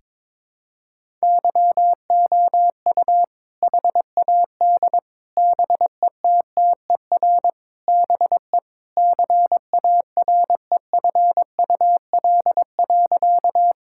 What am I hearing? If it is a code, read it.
Morse code, heard as a keyed tone, about 21 wpm, 700 Hz. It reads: YOU HAD BETTER BE CAREFUL.
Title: Sentences from Top 1600 Words 22wpm